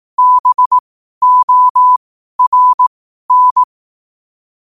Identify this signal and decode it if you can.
Morse code, heard as a keyed tone, about 18 wpm, spaced as for 12 wpm, 1000 Hz. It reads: BORN
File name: qotdfiles / 12wpm